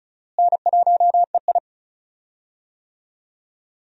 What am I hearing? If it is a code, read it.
Morse code, heard as a keyed tone, about 36 wpm, 700 Hz. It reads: N1EI
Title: USA Call Signs - 4 Characters - Repeated 2x - 35wpm